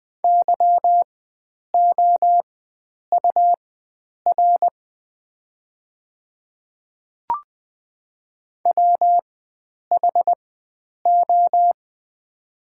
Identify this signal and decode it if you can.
Morse code, heard as a keyed tone, about 20 wpm, spaced as for 9 wpm, 690 Hz. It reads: YOUR WHO
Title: 4x Character Spacing - Top 100 Words - 20wpm